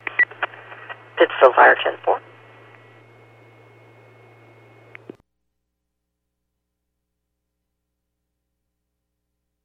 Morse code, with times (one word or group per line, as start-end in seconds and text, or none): none